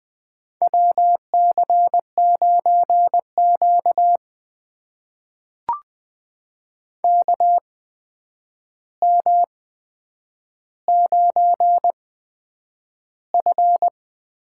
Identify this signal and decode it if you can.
Morse code, heard as a keyed tone, about 20 wpm, spaced as for 9 wpm, 700 Hz. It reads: WC9Q KM9F
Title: USA Call Signs - 4 Characters - 8x Character Spacing - 20wpm